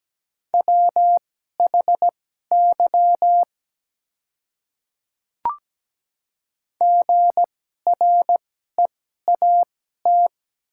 Morse code, heard as a keyed tone, about 17 wpm, spaced as for 12 wpm, 700 Hz. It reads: WHY GREAT
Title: 2x Character Spacing - Top 200 Words - 17wpm